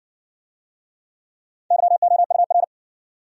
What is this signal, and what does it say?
Morse code, heard as a keyed tone, about 59 wpm, 700 Hz. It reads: YOUR